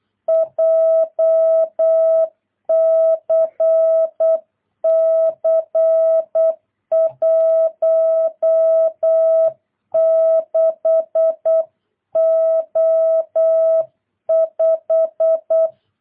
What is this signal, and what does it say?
Morse code, heard as a keyed tone, about 8 wpm, 650 Hz. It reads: JCC 16O5
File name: jh8cbh / 2022cw